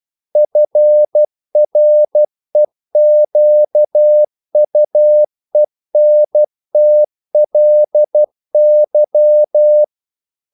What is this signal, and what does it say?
Morse code, heard as a keyed tone, about 12 wpm, 600 Hz. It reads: FREQUENTLY
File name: CW / Practice / EnglishQuotations0112